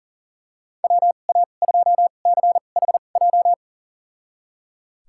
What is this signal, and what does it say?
Morse code, heard as a keyed tone, about 39 wpm, spaced as for 29 wpm, 700 Hz. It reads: WA2CHJ
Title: USA Call Signs - 6 Characters - 2x Character Spacing - 40wpm